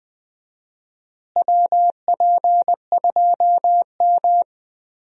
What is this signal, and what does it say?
Morse code, heard as a keyed tone, about 20 wpm, 700 Hz. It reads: WP2M
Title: USA Call Signs - 4 Characters - Repeated 3x - 20wpm